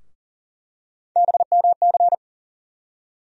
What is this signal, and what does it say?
Morse code, heard as a keyed tone, about 40 wpm, 700 Hz. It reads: BMC
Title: USA Call Sign Postfixes - 3 Letters 40wpm